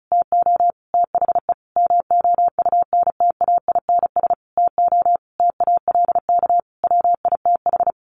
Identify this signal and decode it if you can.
Morse code, heard as a keyed tone, about 34 wpm, 700 Hz. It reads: TO THE MOUNTAINS TO TALK WITH
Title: A Sound Of Thunder 35wpm